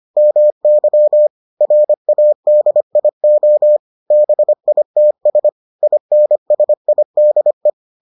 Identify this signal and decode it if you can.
Morse code, heard as a keyed tone, about 25 wpm, 600 Hz. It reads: MY RADIO BITS INSIDE